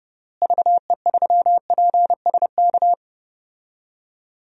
Morse code, heard as a keyed tone, about 30 wpm, 700 Hz. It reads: VE3PSK
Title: DXCC-100 Call Signs - 6 Characters - Repeated 3x - 30wpm